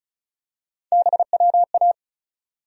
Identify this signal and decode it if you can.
Morse code, heard as a keyed tone, about 34 wpm, 700 Hz. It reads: BWA
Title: USA Call Sign Postfixes - 3 Letters 35wpm